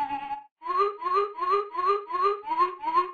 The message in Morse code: none